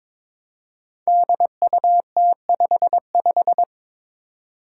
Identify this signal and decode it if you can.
Morse code, heard as a keyed tone, about 22 wpm, 700 Hz. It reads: DUT55
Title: Five Challenging Letter-Number Combinations 22wpm